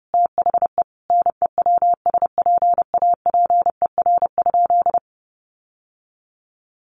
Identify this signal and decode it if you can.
Morse code, heard as a keyed tone, about 28 wpm, 700 Hz. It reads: THE NEWSPAPER?
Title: Sentences from Top 1100 Words 30wpm